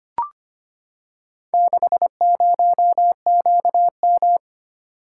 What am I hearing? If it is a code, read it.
Morse code, heard as a keyed tone, about 25 wpm, 700 Hz. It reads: 60QM
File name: practice / 25wpm / Four Letter-Number Combinations - 25wpm